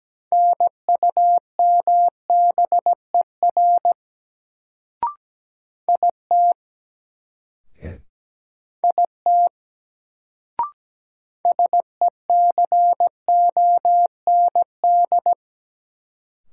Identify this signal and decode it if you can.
Morse code, heard as a keyed tone, about 17 wpm, 700 Hz. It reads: NUMBER IT IT SECOND
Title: Sets of 2 Words - New Word Drill - 17wpm